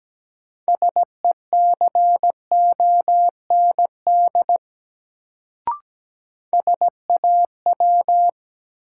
Morse code, heard as a keyed tone, about 17 wpm, 700 Hz. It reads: SECOND SAW
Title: Sentences from Top 300 Words - New Word Drill - 17wpm